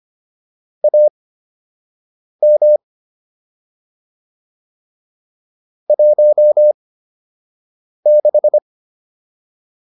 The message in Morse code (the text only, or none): AM 16